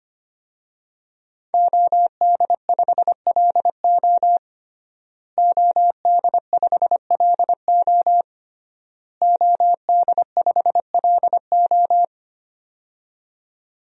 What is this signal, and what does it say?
Morse code, heard as a keyed tone, about 25 wpm, 700 Hz. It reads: OD5LO OD5LO OD5LO
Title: DXCC-100 Call Signs - 5 Characters - Repeated 3x - 25wpm